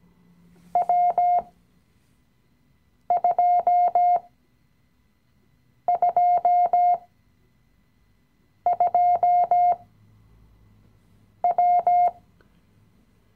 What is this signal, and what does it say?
Morse code, heard as a keyed tone, about 17 wpm, spaced as for 5 wpm, 710 Hz. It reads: W222W
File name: CW-Online-Kurs / Anfänger / Lektion 017